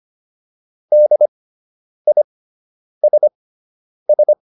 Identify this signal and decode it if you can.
Morse code, heard as a keyed tone, about 25 wpm, spaced as for 9 wpm, 600 Hz. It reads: DISS
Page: fundamental-practice-files